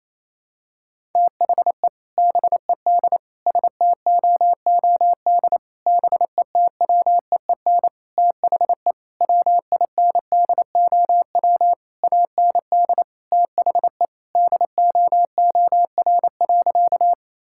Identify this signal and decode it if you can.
Morse code, heard as a keyed tone, about 28 wpm, 700 Hz. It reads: THE BED STOOD BETWEEN THE WINDOW AND THE DOOR.
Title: Sentences from Top 1600 Words 28wpm